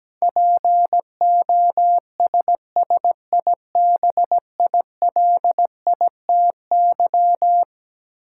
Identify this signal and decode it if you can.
Morse code, heard as a keyed tone, about 17 wpm, 700 Hz. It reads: POSSIBILITY
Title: Sets of 3 Words - New Word Drill - 17wpm